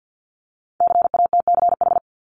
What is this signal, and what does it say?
Morse code, heard as a keyed tone, about 50 wpm, 700 Hz. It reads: CATCH
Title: Top 1251-1500 Words 50wpm